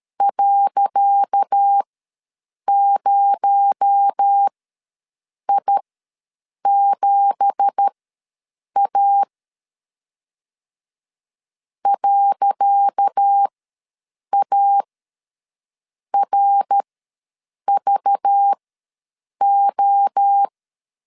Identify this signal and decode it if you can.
Morse code, heard as a keyed tone, about 12 wpm, spaced as for 7 wpm, 800 Hz. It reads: .0I7A .ARVO
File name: LEZIONI / 10 SIGNS / 10C DOT ws 12wpm 3548